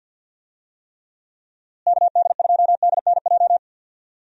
Morse code, heard as a keyed tone, about 51 wpm, 700 Hz. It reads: KD2DNJ
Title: USA Call Signs - 6 Characters - Repeated 2x - 50wpm